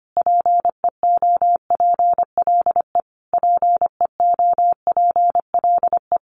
Morse code, heard as a keyed tone, about 25 wpm, 700 Hz. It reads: PEOPLE PEOPLE
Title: Top 500 Words Repeated 25wpm